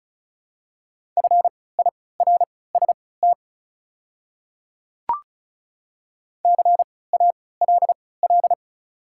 Morse code, heard as a keyed tone, about 36 wpm, spaced as for 20 wpm, 690 Hz. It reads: FIRST CALL
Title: 3x Character Spacing - Top 100 Words - 35wpm